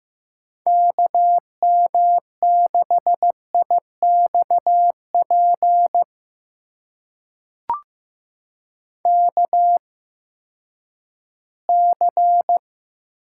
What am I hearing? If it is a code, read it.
Morse code, heard as a keyed tone, about 15 wpm, spaced as for 10 wpm, 700 Hz. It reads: KM6IXP KC
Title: USA Call Signs - 6 Characters - 8x Character Spacing - 15wpm